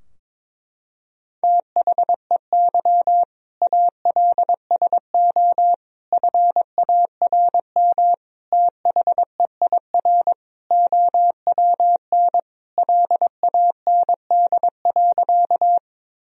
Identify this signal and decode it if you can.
Morse code, heard as a keyed tone, about 22 wpm, 700 Hz. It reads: THEY ALSO FARM THEIR OWN LAND.
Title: Sentences from Top 1000 Words - Review Part 2 - 22wpm